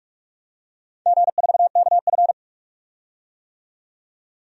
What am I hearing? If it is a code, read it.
Morse code, heard as a keyed tone, about 44 wpm, 700 Hz. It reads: G4KF
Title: DXCC-100 Call Signs - 4 Characters - 45wpm